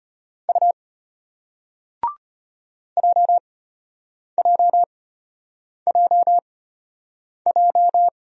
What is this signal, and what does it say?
Morse code, heard as a keyed tone, about 34 wpm, spaced as for 8 wpm, 700 Hz. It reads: U JJJJ